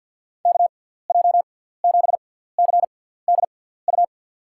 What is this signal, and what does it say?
Morse code, heard as a keyed tone, about 53 wpm, spaced as for 17 wpm, 700 Hz. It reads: KJ7CDU
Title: USA Call Signs - 6 Characters - 6x Character Spacing - 50wpm